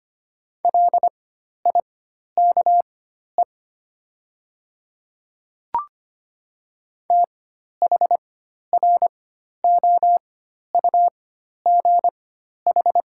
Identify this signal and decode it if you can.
Morse code, heard as a keyed tone, about 25 wpm, spaced as for 12 wpm, 700 Hz. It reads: LIKE THROUGH
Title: 4x Character Spacing - Top 200 Words - 25wpm